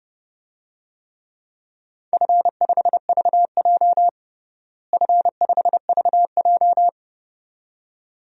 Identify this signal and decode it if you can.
Morse code, heard as a keyed tone, about 30 wpm, 700 Hz. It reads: F5VJ F5VJ
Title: DXCC-100 Call Signs - 4 Characters - Repeated 2x - 30wpm